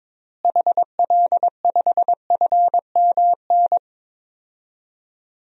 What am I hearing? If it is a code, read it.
Morse code, heard as a keyed tone, about 22 wpm, 700 Hz. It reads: HL5FMN
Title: DXCC-100 Call Signs - 6 Characters - Repeated 2x - 22wpm